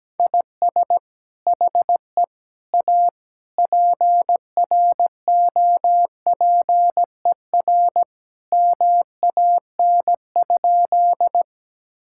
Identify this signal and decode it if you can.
Morse code, heard as a keyed tone, about 17 wpm, 700 Hz. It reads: IS HE A PROPER MAN?
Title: Sentences from Top 1000 Words - Review Part 5 - 17wpm